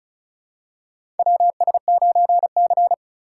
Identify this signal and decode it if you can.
Morse code, heard as a keyed tone, about 36 wpm, 700 Hz. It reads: WS9C
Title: USA Call Signs - 2x Character Spacing - 35wpm